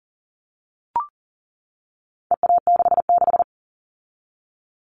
Morse code, heard as a keyed tone, about 40 wpm, 700 Hz. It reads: EA66
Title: All Cut Numbers - 4-Digit 40wpm